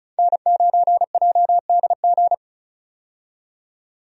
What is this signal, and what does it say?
Morse code, heard as a keyed tone, about 36 wpm, 700 Hz. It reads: N9JDG
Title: USA Call Signs - 5 Characters - 35wpm